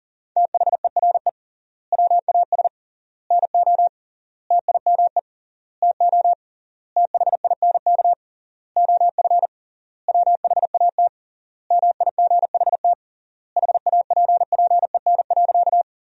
Morse code, heard as a keyed tone, about 40 wpm, 700 Hz. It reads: THERE WAS NO TIME TO THINK OF WHAT MIGHT HAPPEN.
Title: 3x Word-Spacing - Sentences from Top 400 Words - 40wpm